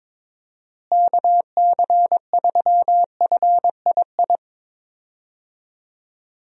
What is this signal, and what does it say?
Morse code, heard as a keyed tone, about 22 wpm, 700 Hz. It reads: KC3FII
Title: USA Call Signs - 6 Characters - Repeated 2x - 22wpm